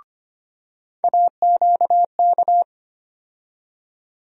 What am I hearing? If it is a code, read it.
Morse code, heard as a keyed tone, about 25 wpm, 700 Hz. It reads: AQK
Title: USA Call Sign Postfixes - 3 Letters 25wpm